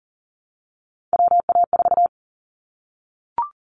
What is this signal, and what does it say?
Morse code, heard as a keyed tone, about 40 wpm, 700 Hz. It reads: WA4
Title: Callsign Prefixes for Top 25 DXCC - Repeated 3x - 40wpm